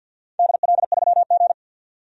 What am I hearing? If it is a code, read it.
Morse code, heard as a keyed tone, about 49 wpm, 700 Hz. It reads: DL3G
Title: DXCC-100 Call Signs - 4 Characters - 50wpm